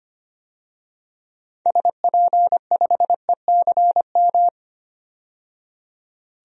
Morse code, heard as a keyed tone, about 25 wpm, 700 Hz. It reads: SP5ECM
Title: DXCC-100 Call Signs - 6 Characters - 25wpm